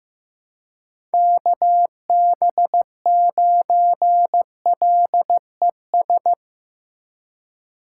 Morse code, heard as a keyed tone, about 15 wpm, 700 Hz. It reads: KB9LES
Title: USA Call Signs - 6 Characters - 3x Character Spacing - 15wpm